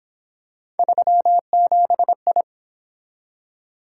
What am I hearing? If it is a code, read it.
Morse code, heard as a keyed tone, about 26 wpm, 700 Hz. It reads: 37I
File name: Speed-Racing - Three Letter-Number Combinations - 22wpm